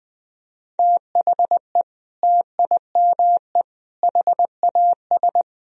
Morse code, heard as a keyed tone, about 20 wpm, 700 Hz. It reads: THE TIME HAS
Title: Sentences from Top 1000 Words - Review Part 1 - 20wpm